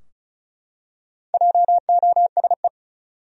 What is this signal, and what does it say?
Morse code, heard as a keyed tone, about 36 wpm, 700 Hz. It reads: JOSE
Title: USA Common Names - Max 4 Letters - 35wpm